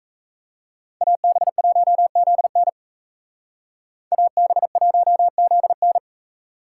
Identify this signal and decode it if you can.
Morse code, heard as a keyed tone, about 36 wpm, 700 Hz. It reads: AB1ZN AB1ZN